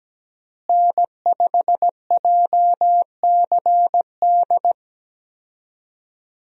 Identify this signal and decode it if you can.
Morse code, heard as a keyed tone, about 17 wpm, 700 Hz. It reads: N5JCD
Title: USA Call Signs - 5 Characters - Repeated 2x - 17wpm